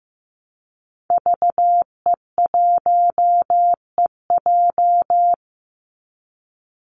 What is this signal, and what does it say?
Morse code, heard as a keyed tone, about 15 wpm, 700 Hz. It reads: VE1EJ
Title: Call Signs Repeated 3x - 15wpm